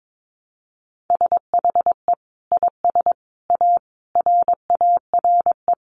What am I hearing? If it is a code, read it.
Morse code, heard as a keyed tone, about 22 wpm, 700 Hz. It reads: SHE IS A RARE
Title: Sentences from Top 1100 Words 22wpm